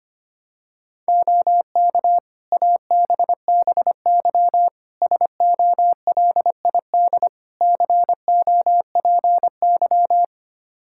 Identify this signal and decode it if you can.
Morse code, heard as a keyed tone, about 25 wpm, 700 Hz. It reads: OK ABBY SOLID COPY
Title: Practice QSO - Exchange 3 - More Info - 25wpm